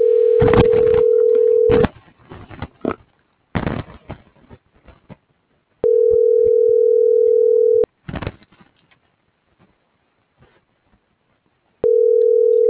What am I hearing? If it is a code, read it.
Ringback tone (North America), 440 + 480 Hz.